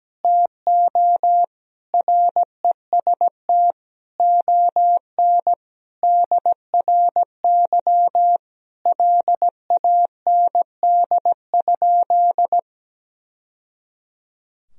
Morse code, heard as a keyed tone, about 17 wpm, 700 Hz. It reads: TO REST ON DRY LAND?
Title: Sentences from Top 1000 Words - Review Part 3 - 17wpm